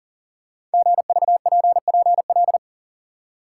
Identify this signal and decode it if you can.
Morse code, heard as a keyed tone, about 42 wpm, 700 Hz. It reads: GVPPL
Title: Code Groups - 5 Letters - 40wpm